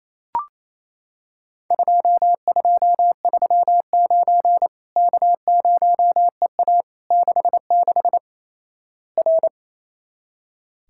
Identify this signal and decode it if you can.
Morse code, heard as a keyed tone, about 28 wpm, 700 Hz. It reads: 2239 K0EA 66 R
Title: Worked All Europe DX Contest - QTC Exchange - 28wpm